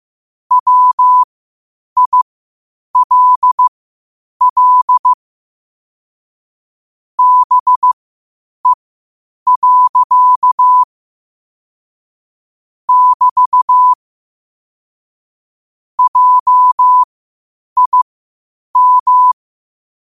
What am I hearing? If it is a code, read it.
Morse code, heard as a keyed tone, about 15 wpm, spaced as for 8 wpm, 1000 Hz. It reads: WILL BE. <BT> JIM
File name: qotdfiles / 8wpm